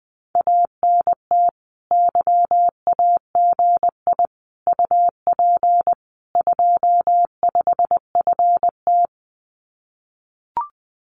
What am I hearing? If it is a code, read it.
Morse code, heard as a keyed tone, about 20 wpm, 700 Hz. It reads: ANT YAGI UP 25FT
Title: Common QSO Phrases Encore 20wpm